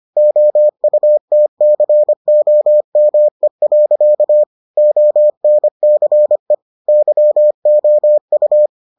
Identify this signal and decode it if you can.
Morse code, heard as a keyed tone, about 25 wpm, 600 Hz. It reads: OUTCOME. ONCE YOU